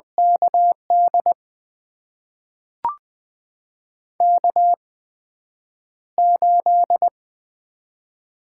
Morse code, heard as a keyed tone, about 20 wpm, spaced as for 9 wpm, 700 Hz. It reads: KD K8
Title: USA Call Signs - 4 Characters - 8x Character Spacing - 20wpm